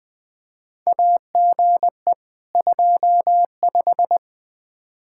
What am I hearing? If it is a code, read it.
Morse code, heard as a keyed tone, about 21 wpm, 700 Hz. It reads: AGE 25
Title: Practice QSO - Exchange 3 - More Info - 20wpm